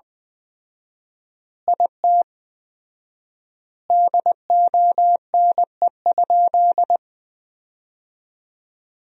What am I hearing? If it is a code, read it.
Morse code, heard as a keyed tone, about 20 wpm, 700 Hz. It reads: IT DONE?
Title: 4x Word-Spacing - Sentences from Top 500 Words - 20wpm